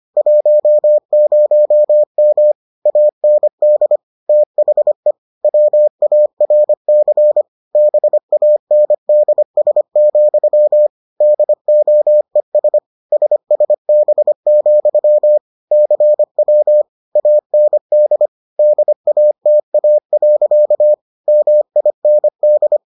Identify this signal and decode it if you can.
Morse code, heard as a keyed tone, about 25 wpm, 600 Hz. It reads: 10M AND THE WARC BANDS, DOES SSB, CW AND DATA. MIND